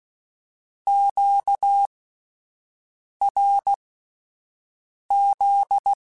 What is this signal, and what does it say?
Morse code, heard as a keyed tone, about 16 wpm, spaced as for 5 wpm, 800 Hz. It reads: QRZ